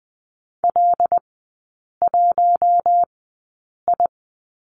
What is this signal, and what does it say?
Morse code, heard as a keyed tone, about 20 wpm, spaced as for 8 wpm, 700 Hz. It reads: L1I
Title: 074 Lesson 13e - Letter L - Three Characters 20wpm